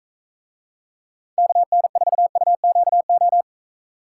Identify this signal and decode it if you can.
Morse code, heard as a keyed tone, about 44 wpm, 700 Hz. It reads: KN4UQO